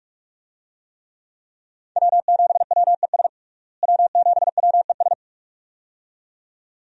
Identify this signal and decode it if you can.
Morse code, heard as a keyed tone, about 46 wpm, 700 Hz. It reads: W7WES W7WES
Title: USA Call Signs - 5 Characters - Repeated 2x - 45wpm